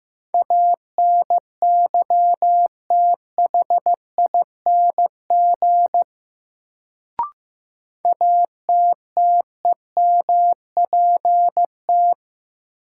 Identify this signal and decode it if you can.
Morse code, heard as a keyed tone, about 15 wpm, 700 Hz. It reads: ANYTHING ATTEMPT
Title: Sets of 4 Words - New Word Drill - 15wpm